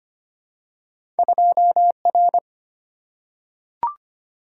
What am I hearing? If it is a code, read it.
Morse code, heard as a keyed tone, about 25 wpm, 700 Hz. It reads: 2 R
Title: Single-Letter Single-Number 25wpm